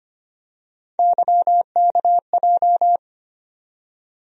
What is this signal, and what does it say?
Morse code, heard as a keyed tone, about 25 wpm, 700 Hz. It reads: YKJ
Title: USA Call Sign Postfixes - 3 Letters 25wpm